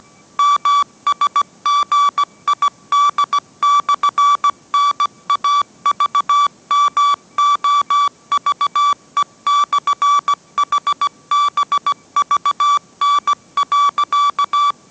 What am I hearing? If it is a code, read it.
Morse code, heard as a keyed tone, about 17 wpm, 1150 Hz. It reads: MSGID/NAVMOVE/HBVN.